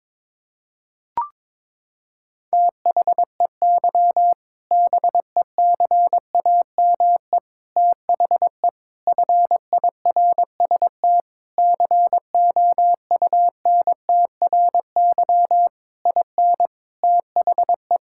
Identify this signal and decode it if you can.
Morse code, heard as a keyed tone, about 22 wpm, 700 Hz. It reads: THEY BECAME THE FIRST COUNTRY IN THE
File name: practice / 22wpm / Sentences with 8-Word Phrases - 22wpm